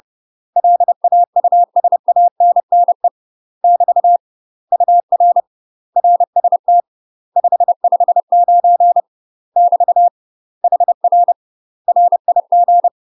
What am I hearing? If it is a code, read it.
Morse code, heard as a keyed tone, about 30 wpm, 700 Hz. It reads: LAUSANNE <BT> UR RST 559 <BT> HR RIG
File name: UK MASTER QSOs TRAINING / 30WPM / UK MASTER QSOs 0041